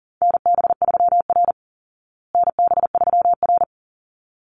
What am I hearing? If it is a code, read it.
Morse code, heard as a keyed tone, about 40 wpm, 700 Hz. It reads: NB3R NB3R
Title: Call Signs Repeated 3x - 40wpm